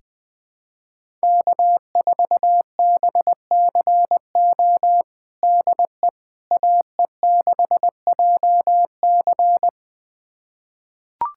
Morse code, heard as a keyed tone, about 20 wpm, 700 Hz. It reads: K4BCO DE AE6JC
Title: USA Callsign DE Callsign - 5 Characters - 20wpm